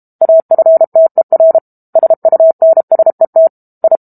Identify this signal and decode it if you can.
Morse code, heard as a keyed tone, about 32 wpm, 650 Hz. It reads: AFTER SUNSET I
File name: Book Two / 06-Chapter 6 - 32 WPM